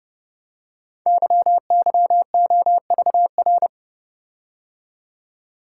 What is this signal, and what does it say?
Morse code, heard as a keyed tone, about 30 wpm, 700 Hz. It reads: YYOVR
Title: Code Groups - 5 Letters - 30wpm